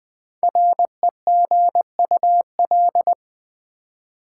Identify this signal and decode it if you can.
Morse code, heard as a keyed tone, about 20 wpm, 700 Hz. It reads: REGUL
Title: Build A Word - 5 Letters At A Time - 10-Letter Words - 20wpm